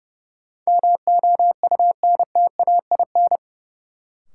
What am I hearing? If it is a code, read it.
Morse code, heard as a keyed tone, about 31 wpm, 700 Hz. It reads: MOUNTAIN
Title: Sentences from Top 1000 Words - Review Part 1 - New Word Drill - 30wpm